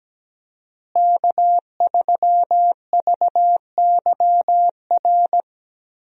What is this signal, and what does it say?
Morse code, heard as a keyed tone, about 17 wpm, 700 Hz. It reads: K3VYR
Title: USA Call Signs - 5 Characters - Repeated 3x - 17wpm